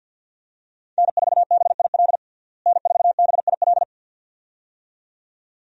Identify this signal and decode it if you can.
Morse code, heard as a keyed tone, about 51 wpm, 700 Hz. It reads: N4BIL N4BIL
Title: USA Call Signs - 5 Characters - Repeated 2x - 50wpm